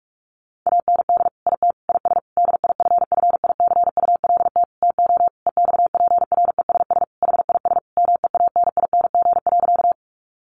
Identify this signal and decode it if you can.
Morse code, heard as a keyed tone, about 44 wpm, 700 Hz. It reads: AND IT IS DIFFICULT TO EXPRESS HIS MEANING.
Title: Sentences from Top 1100 Words 45wpm